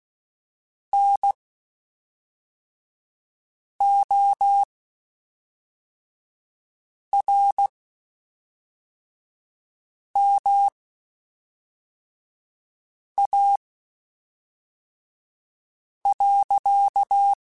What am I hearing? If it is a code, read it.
Morse code, heard as a keyed tone, about 16 wpm, spaced as for 3 wpm, 800 Hz. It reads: NORMA.
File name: (spaced) / QSOs Spaced-mp3s / WD0FBT 1 @5 sp